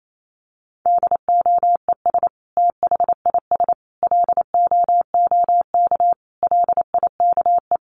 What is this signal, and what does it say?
Morse code, heard as a keyed tone, about 28 wpm, 700 Hz. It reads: DOES THIS LOOK LIKE
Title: Sentences from Top 1500 Words 28wpm